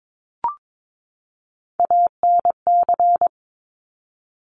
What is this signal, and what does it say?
Morse code, heard as a keyed tone, about 22 wpm, 700 Hz. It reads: ANC
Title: Top Three Letter Combinations in Words 22wpm